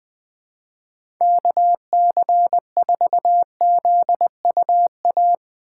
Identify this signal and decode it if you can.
Morse code, heard as a keyed tone, about 20 wpm, 700 Hz. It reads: KC4ZUA
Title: 2023 POTA Hunter Call Signs - 20wpm